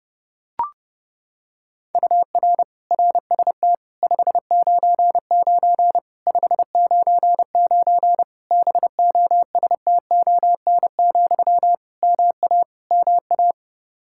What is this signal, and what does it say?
Morse code, heard as a keyed tone, about 30 wpm, 700 Hz. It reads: UR RST 599 599 BOSTON, MA MA
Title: Practice QSO - Exchange 1 - Beginning - 30wpm